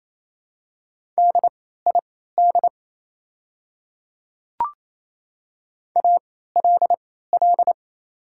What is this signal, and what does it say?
Morse code, heard as a keyed tone, about 28 wpm, spaced as for 16 wpm, 690 Hz. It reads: DID ALL
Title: 3x Character Spacing - Top 100 Words - 28wpm